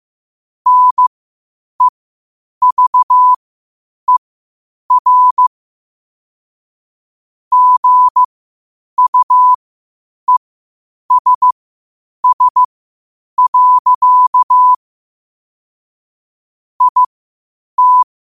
Morse code, heard as a keyed tone, about 15 wpm, spaced as for 8 wpm, 1000 Hz. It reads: NEVER GUESS. IT
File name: qotdfiles / 8wpm